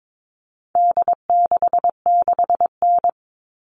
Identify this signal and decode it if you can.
Morse code, heard as a keyed tone, about 22 wpm, 700 Hz. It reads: D66N
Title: All Cut Numbers - 4-Digit 22wpm